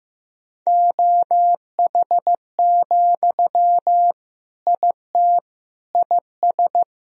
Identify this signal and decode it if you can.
Morse code, heard as a keyed tone, about 15 wpm, 700 Hz. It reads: OH, IT IS
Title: Sentences from Top 1000 Words - Review Part 3 - 15wpm